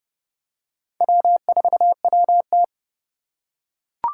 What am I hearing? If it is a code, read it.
Morse code, heard as a keyed tone, about 32 wpm, 700 Hz. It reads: W4WT
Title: USA Call Signs - 4 Characters - 2x Character Spacing - 30wpm